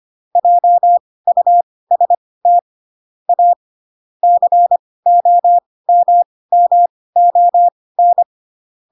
Morse code, heard as a keyed tone, about 25 wpm, spaced as for 18 wpm, 700 Hz. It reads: JUST A COMMON
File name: Pinocchio / Pinocchio ch1 12 25 ik1hge